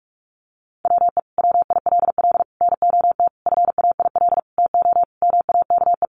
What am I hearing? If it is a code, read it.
Morse code, heard as a keyed tone, about 44 wpm, 700 Hz. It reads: WE WILL NOT FAIL TO MAKE
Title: Sentences from Top 1100 Words 45wpm